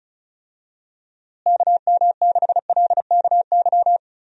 Morse code, heard as a keyed tone, about 36 wpm, 700 Hz. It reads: KM6LKY
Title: USA Call Signs - 6 Characters - Repeated 2x - 35wpm